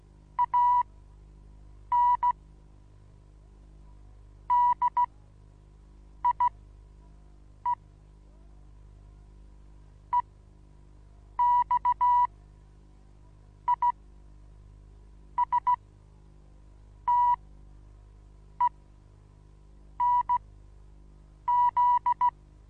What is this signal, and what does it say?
Morse code, heard as a keyed tone, about 16 wpm, spaced as for 6 wpm, 980 Hz. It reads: AN DIE EXISTENZ
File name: Onlinelehrgang / morsen / DJ4UF Morsen25